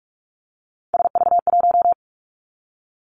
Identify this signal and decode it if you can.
Morse code, heard as a keyed tone, about 46 wpm, 700 Hz. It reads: SV1
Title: Callsign Prefixes for Top 50 DXCC - 45wpm